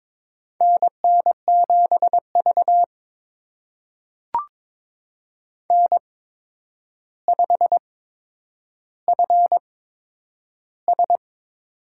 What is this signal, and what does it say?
Morse code, heard as a keyed tone, about 22 wpm, spaced as for 9 wpm, 700 Hz. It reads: NN7V N5FS
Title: USA Call Signs - 4 Characters - 8x Character Spacing - 22wpm